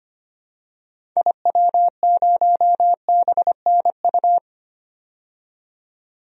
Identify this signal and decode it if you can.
Morse code, heard as a keyed tone, about 25 wpm, 700 Hz. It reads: IW0BNU
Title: DXCC-100 Call Signs - 6 Characters - 25wpm